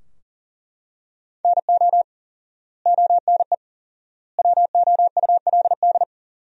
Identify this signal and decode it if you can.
Morse code, heard as a keyed tone, about 37 wpm, 700 Hz. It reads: NO ONE WOULD
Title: 4x Word-Spacing - Sentences from Top 300 Words - 40wpm